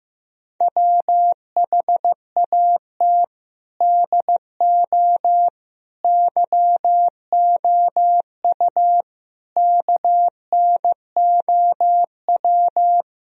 Morse code, heard as a keyed tone, about 15 wpm, 700 Hz. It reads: WHAT DO YOU KNOW
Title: Sentences from Top 1000 Words - Review Part 2 - 15wpm